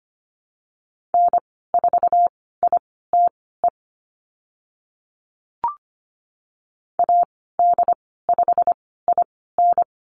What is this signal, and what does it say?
Morse code, heard as a keyed tone, about 25 wpm, spaced as for 16 wpm, 700 Hz. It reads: N4ITE AD5IN
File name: Lesson 15g - Number 5 - Call Signs 25wpm